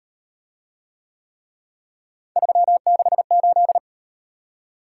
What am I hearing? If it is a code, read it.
Morse code, heard as a keyed tone, about 38 wpm, 700 Hz. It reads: 368